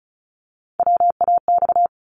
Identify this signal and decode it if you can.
Morse code, heard as a keyed tone, about 36 wpm, 700 Hz. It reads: WAX
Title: Multi-letter Phonemes 35wpm